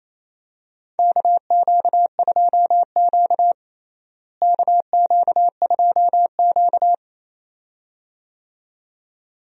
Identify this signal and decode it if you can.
Morse code, heard as a keyed tone, about 28 wpm, 700 Hz. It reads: KQ2Q KQ2Q
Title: USA Call Signs - 4 Characters - Repeated 2x - 28wpm